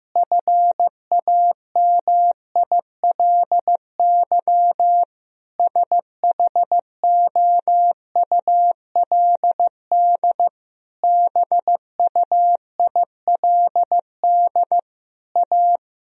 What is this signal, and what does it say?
Morse code, heard as a keyed tone, about 15 wpm, 700 Hz. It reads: FAMILY SHOULD BUILD A
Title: Sentences from Top 1000 Words - Review Part 1 - 15wpm